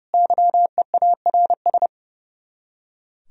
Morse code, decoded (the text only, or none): YEARS